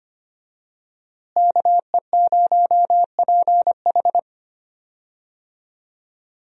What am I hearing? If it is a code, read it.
Morse code, heard as a keyed tone, about 25 wpm, 700 Hz. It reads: KE0PH
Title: USA Call Signs - 5 Characters - Repeated 3x - 25wpm